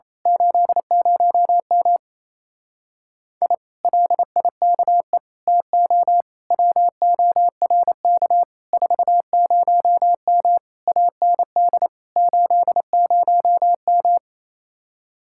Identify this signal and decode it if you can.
Morse code, heard as a keyed tone, about 27 wpm, 700 Hz. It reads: 80M I LIKE TO WORK 40M AND 80M
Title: Speed-Racing Common QSO Phrases - 28wpm